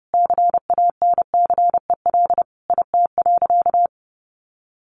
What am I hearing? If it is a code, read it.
Morse code, heard as a keyed tone, about 30 wpm, 700 Hz. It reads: CANCEL IT.
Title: Sentences from Top 1200 Words 30wpm